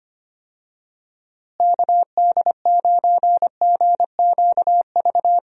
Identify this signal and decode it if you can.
Morse code, heard as a keyed tone, about 25 wpm, 700 Hz. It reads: KD9GQV
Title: USA Call Signs - 6 Characters - Repeated 3x - 25wpm